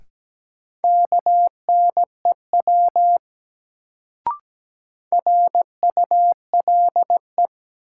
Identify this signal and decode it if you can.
Morse code, heard as a keyed tone, about 17 wpm, 700 Hz. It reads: KNEW RULE
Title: Sentences from Top 400 Words - New Word Drill - 17wpm